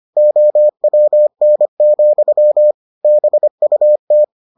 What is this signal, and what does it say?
Morse code, heard as a keyed tone, about 25 wpm, 600 Hz. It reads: OWN, BUT